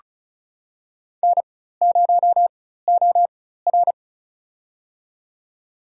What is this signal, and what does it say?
Morse code, heard as a keyed tone, about 34 wpm, spaced as for 16 wpm, 700 Hz. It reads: N0OR
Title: USA Call Signs - 4 Characters - 4x Character Spacing - 35wpm